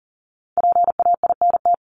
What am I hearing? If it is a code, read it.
Morse code, heard as a keyed tone, about 40 wpm, 700 Hz. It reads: PAINT
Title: Top 751-1000 Words 40wpm